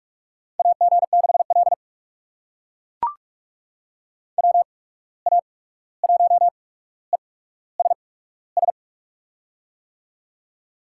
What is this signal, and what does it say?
Morse code, heard as a keyed tone, about 44 wpm, spaced as for 17 wpm, 700 Hz. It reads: AG6L WA1ESS
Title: USA Call Signs - 8x Character Spacing - 45wpm